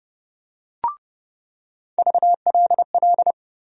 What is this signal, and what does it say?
Morse code, heard as a keyed tone, about 32 wpm, 700 Hz. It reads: VLL
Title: USA Call Sign Postfixes - 3 Letters 30wpm